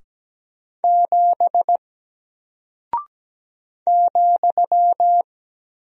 Morse code, heard as a keyed tone, about 17 wpm, 700 Hz. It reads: 7 ,
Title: Single Character (Letter:Number:Punctuation:Prosign) Alt 17wpm